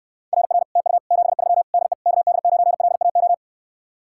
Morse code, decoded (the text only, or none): RR TU 73 DE KD9LAO